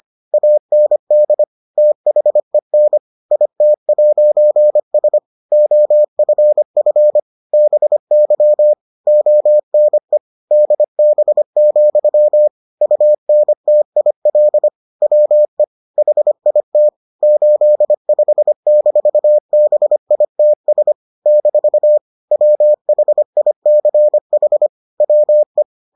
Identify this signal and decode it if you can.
Morse code, heard as a keyed tone, about 25 wpm, 600 Hz. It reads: AND THEN IT'S OFF BY ONE DB, UNTIL WE HIT 85-BITS - WHICH WE